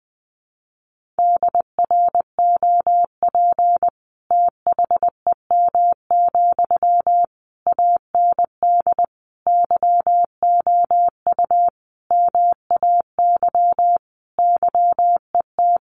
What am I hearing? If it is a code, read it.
Morse code, heard as a keyed tone, about 20 wpm, 700 Hz. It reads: DROP THEM, AND YOU MAY YET
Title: Sentences from Top 1400 Words 20wpm